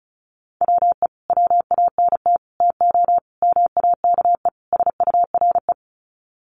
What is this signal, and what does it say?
Morse code, heard as a keyed tone, about 34 wpm, 700 Hz. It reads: WE WANT TO MAKE SURE
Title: Sets of 5 Words Encore2 35wpm